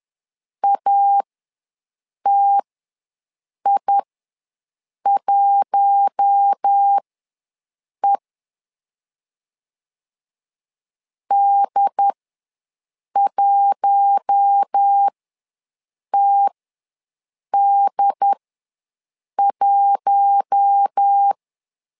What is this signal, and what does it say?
Morse code, heard as a keyed tone, about 10 wpm, spaced as for 6 wpm, 800 Hz. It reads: ATI1E D1TD1
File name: LEZIONI / 4 Q1 ws 10wpm 2538